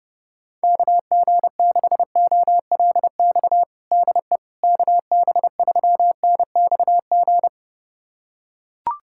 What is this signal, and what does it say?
Morse code, heard as a keyed tone, about 30 wpm, 700 Hz. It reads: KG6OLX DE KB3NXG
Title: USA Callsign DE Callsign - 6 Characters - 30wpm